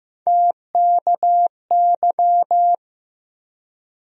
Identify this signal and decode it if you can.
Morse code, heard as a keyed tone, about 15 wpm, 700 Hz. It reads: TKY
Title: USA Call Sign Postfixes - 3 Letters 15wpm